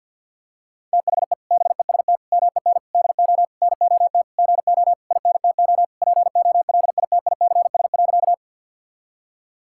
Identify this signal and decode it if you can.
Morse code, heard as a keyed tone, about 48 wpm, 700 Hz. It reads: THE BEST MEN DO NOT GO INTO POLITICS.
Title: Sentences from Top 1000 Words - Review Part 6 - 50wpm